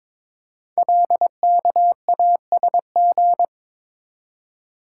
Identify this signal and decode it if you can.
Morse code, heard as a keyed tone, about 22 wpm, 700 Hz. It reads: LKASG
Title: Code Groups - 5 Letters - 22wpm